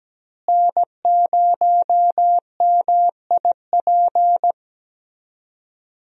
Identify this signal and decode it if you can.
Morse code, heard as a keyed tone, about 17 wpm, 700 Hz. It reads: N0MIP
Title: USA Call Signs - 5 Characters - Repeated 2x - 17wpm